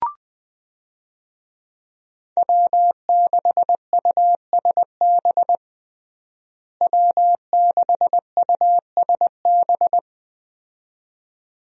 Morse code, heard as a keyed tone, about 20 wpm, 700 Hz. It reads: W6USB W6USB
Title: USA Call Signs - 5 Characters - Repeated 2x - 20wpm